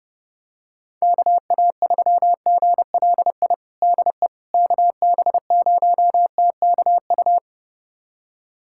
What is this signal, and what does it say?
Morse code, heard as a keyed tone, about 32 wpm, 700 Hz. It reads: KA3GLI DE KB0TKU
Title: USA Callsign DE Callsign - 6 Characters - 30wpm